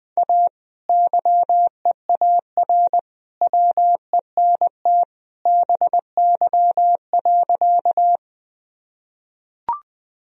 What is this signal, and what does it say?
Morse code, heard as a keyed tone, about 20 wpm, 700 Hz. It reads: A YEAR WENT BY.
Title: Sentences from Top 1000 Words - Review Part 1 - 20wpm